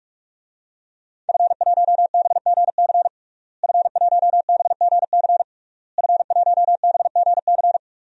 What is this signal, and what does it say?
Morse code, heard as a keyed tone, about 41 wpm, 700 Hz. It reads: F1BGC F1BGC F1BGC
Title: DXCC-100 Call Signs - 5 Characters - Repeated 3x - 45wpm